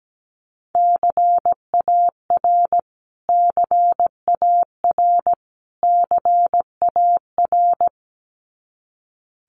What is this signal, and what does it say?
Morse code, heard as a keyed tone, about 17 wpm, 700 Hz. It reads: CAR CAR CAR
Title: Top 500 Words Repeated 17wpm